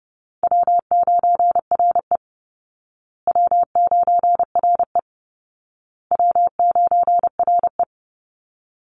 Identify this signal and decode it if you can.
Morse code, heard as a keyed tone, about 30 wpm, 700 Hz. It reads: W9RE W9RE W9RE
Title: Easy Call Signs Repeated 3x - 30wpm